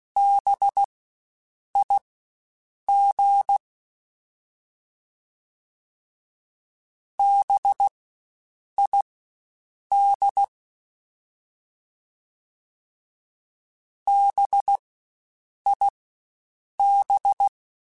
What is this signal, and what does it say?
Morse code, heard as a keyed tone, about 16 wpm, spaced as for 7 wpm, 800 Hz. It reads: BIG BID BIB
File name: Spaced Alphas / Spaced Alphas mp3s / Bs sp @15